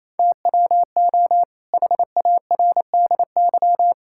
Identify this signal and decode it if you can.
Morse code, heard as a keyed tone, about 27 wpm, 700 Hz. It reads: TWO HARDY
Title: Pangram Challenge v2 - 28wpm